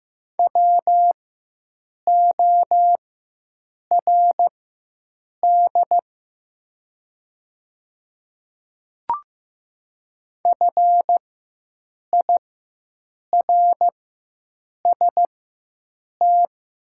Morse code, heard as a keyed tone, about 15 wpm, spaced as for 7 wpm, 700 Hz. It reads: WORD FIRST
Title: 4x Character Spacing - Top 100 Words - 15wpm